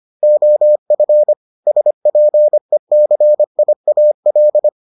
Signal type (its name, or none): Morse code